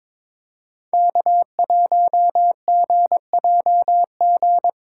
Morse code, heard as a keyed tone, about 22 wpm, 700 Hz. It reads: K1GJG
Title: USA Call Signs - 5 Characters - 3x Character Spacing - 22wpm